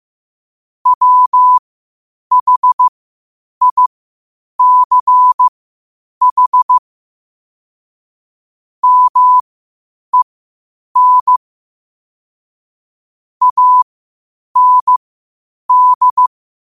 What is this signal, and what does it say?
Morse code, heard as a keyed tone, about 15 wpm, spaced as for 8 wpm, 1000 Hz. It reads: WHICH MEN AND
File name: qotdfiles / 8wpm